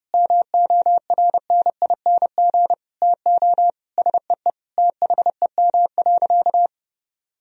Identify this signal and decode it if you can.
Morse code, heard as a keyed tone, about 32 wpm, 700 Hz. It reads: MORNING TO SEE THEM.
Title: Sentences from Top 1000 Words - Review Part 1 - 30wpm